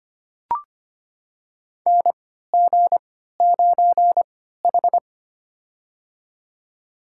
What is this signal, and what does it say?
Morse code, heard as a keyed tone, about 25 wpm, spaced as for 14 wpm, 700 Hz. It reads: NG9H
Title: USA Call Signs - 4 Characters - 3x Character Spacing - 25wpm